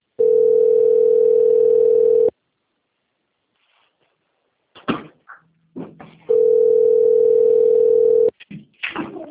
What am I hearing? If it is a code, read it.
Ringback tone (North America), 440 + 480 Hz.